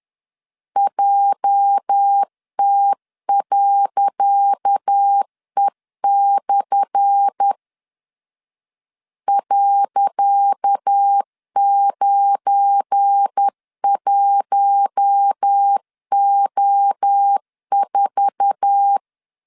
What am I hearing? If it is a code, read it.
Morse code, heard as a keyed tone, about 10 wpm, 800 Hz. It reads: JT.E/ .91O4
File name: LEZIONI / 10 SIGNS / 10C DOT 10wpm 2940